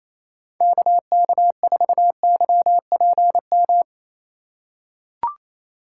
Morse code, heard as a keyed tone, about 28 wpm, 700 Hz. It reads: KK4YPM
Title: USA Call Signs - 6 Characters - 2x Character Spacing - 28wpm